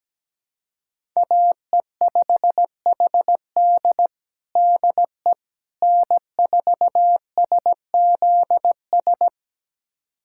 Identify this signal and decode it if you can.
Morse code, heard as a keyed tone, about 17 wpm, 700 Hz. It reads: AE5HD DE N4SZS
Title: USA Callsign DE Callsign - 5 Characters - 17wpm